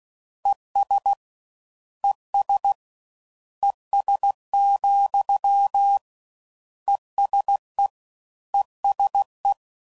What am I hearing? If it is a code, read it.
Morse code, heard as a keyed tone, about 16 wpm, 800 Hz. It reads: ES ES ES, ESE ESE